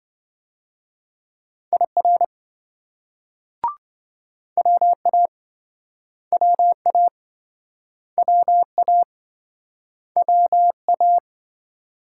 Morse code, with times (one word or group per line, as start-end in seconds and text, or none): 1.7-2.2 IR
4.6-5.3 WA
6.3-7.1 WA
8.2-9.0 WA
10.2-11.2 WA